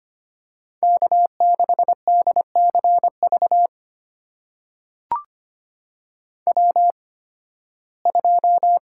Morse code, heard as a keyed tone, about 25 wpm, spaced as for 16 wpm, 700 Hz. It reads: K6DCV W2
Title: USA Call Signs - 5 Characters - 8x Character Spacing - 25wpm